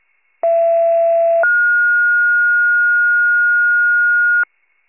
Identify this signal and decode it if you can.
Two-tone page, heard as a one-second tone, then a three-second tone, about 670 then 1430 Hz. It is heard from 0.4 to 4.4 s.
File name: wcalert